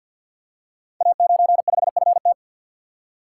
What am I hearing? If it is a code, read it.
Morse code, heard as a keyed tone, about 52 wpm, 700 Hz. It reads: A95FT
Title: Code Groups - 5 Letter:Numbers - 50wpm